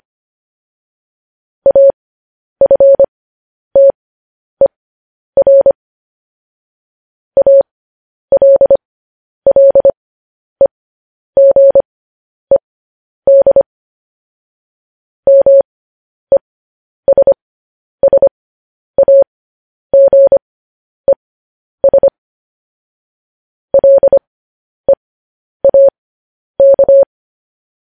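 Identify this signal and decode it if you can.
Morse code, heard as a keyed tone, about 25 wpm, spaced as for 10 wpm, 550 Hz. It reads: AFTER ALLEGED MESSAGES LEAK
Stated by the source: Morse news headlines at 10.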